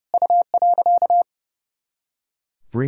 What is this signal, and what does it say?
Morse code, heard as a keyed tone, about 32 wpm, 700 Hz. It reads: U.
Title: Sentences from Top 500 Words and Abbreviations 30wpm